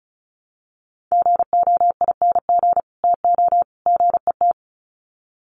Morse code, heard as a keyed tone, about 34 wpm, 700 Hz. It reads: GOING TO GET
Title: Sets of 3 Words 35wpm